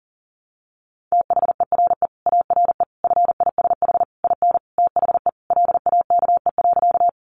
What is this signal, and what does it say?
Morse code, heard as a keyed tone, about 40 wpm, 700 Hz. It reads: THERE ARE FISH IN THE LAKE.
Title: Sentences from Top 1100 Words 40wpm